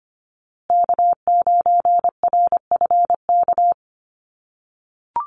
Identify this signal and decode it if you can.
Morse code, heard as a keyed tone, about 25 wpm, 700 Hz. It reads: K9RFK
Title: USA Call Signs Repeated 3x 25wpm